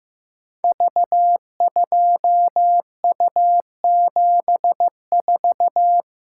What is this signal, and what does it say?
Morse code, heard as a keyed tone, about 15 wpm, 700 Hz. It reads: V2U74
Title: Code Groups - 5 Letter:Numbers - 15wpm